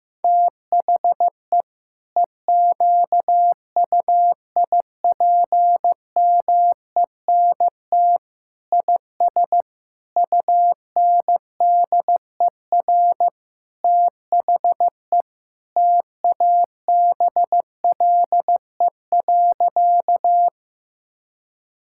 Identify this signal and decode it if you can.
Morse code, heard as a keyed tone, about 15 wpm, 700 Hz. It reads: THE EQUIPMENT IS UNDER THE TABLE.
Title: Sentences from Top 1000 Words - Review Part 5 - 15wpm